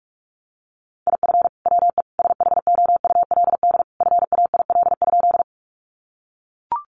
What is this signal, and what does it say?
Morse code, heard as a keyed tone, about 44 wpm, 700 Hz. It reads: IF WE SHOULD FAIL?
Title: Sentences from Top 1100 Words 45wpm